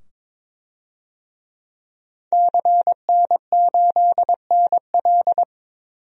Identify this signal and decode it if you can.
Morse code, heard as a keyed tone, about 22 wpm, 700 Hz. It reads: CN8NL
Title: DXCC-100 Call Signs - 5 Characters - 22wpm